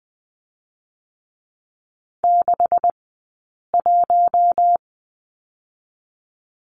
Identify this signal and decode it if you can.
Morse code, heard as a keyed tone, about 20 wpm, spaced as for 8 wpm, 700 Hz. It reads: 61
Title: 126 Lesson 21d - Number 6 - Two Characters 20wpm